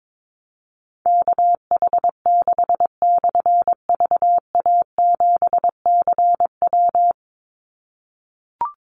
Morse code, heard as a keyed tone, about 22 wpm, 700 Hz. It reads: KH6/VA7CW
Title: Call Signs - 22wpm